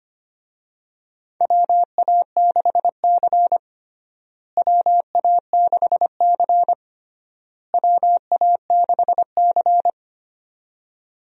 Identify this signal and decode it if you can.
Morse code, heard as a keyed tone, about 25 wpm, 700 Hz. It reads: WA6C WA6C WA6C
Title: USA Call Signs - 4 Characters - Repeated 3x - 25wpm